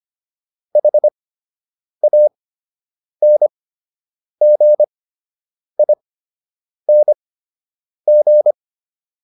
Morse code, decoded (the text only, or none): HANGING